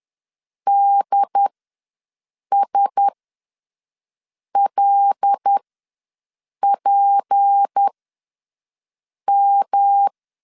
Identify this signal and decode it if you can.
Morse code, heard as a keyed tone, about 10 wpm, spaced as for 5 wpm, 800 Hz. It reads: DSLPM
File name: LEZIONI / 8 YJ45 / 8 YJ4 ws 10wpm 4020